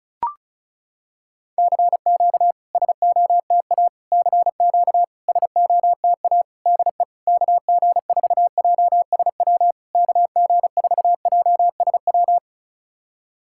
Morse code, heard as a keyed tone, about 36 wpm, 690 Hz. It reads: CQ SOTA CQ SOTA DE KG4JSW KG4JSW
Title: SOTA QSO Practice 35wpm